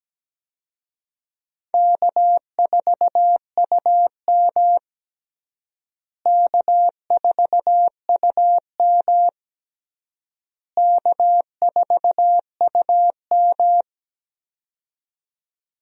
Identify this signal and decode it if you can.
Morse code, heard as a keyed tone, about 17 wpm, 700 Hz. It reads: K4UM K4UM K4UM
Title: USA Call Signs - 4 Characters - Repeated 3x - 17wpm